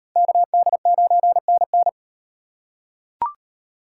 Morse code, heard as a keyed tone, about 38 wpm, 700 Hz. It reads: KD9NN